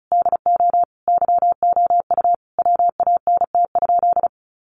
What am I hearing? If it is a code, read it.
Morse code, heard as a keyed tone, about 34 wpm, 700 Hz. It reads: DO YOU WANT?
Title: Sentences from Top 1100 Words 35wpm